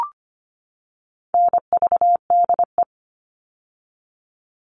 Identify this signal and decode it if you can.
Morse code, heard as a keyed tone, about 25 wpm, 700 Hz. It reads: NVDE
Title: All Cut Numbers - 4-Digit 25wpm